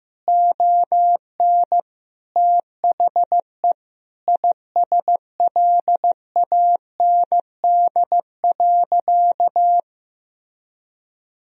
Morse code, heard as a keyed tone, about 15 wpm, 700 Hz. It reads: ON THE ISLAND.
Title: Sentences from Top 1000 Words - Review Part 3 - 15wpm